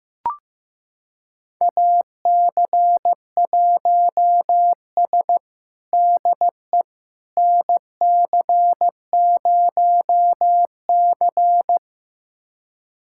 Morse code, heard as a keyed tone, about 15 wpm, 700 Hz. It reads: AC1S DE NC0C
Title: USA Callsign DE Callsign - 4 Characters - 15wpm